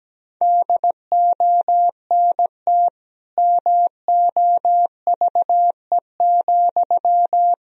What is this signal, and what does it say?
Morse code, heard as a keyed tone, about 17 wpm, 700 Hz. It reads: DONT MOVE,
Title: Sentences from Top 300 Words and Abbreviations 17wpm